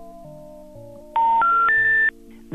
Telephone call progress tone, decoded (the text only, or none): special information tone